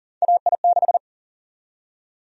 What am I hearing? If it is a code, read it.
Morse code, heard as a keyed tone, about 40 wpm, 700 Hz. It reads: AI6
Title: USA Call Sign Prefixes - 40wpm